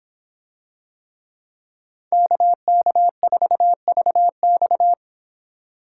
Morse code, heard as a keyed tone, about 26 wpm, 700 Hz. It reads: KK4VX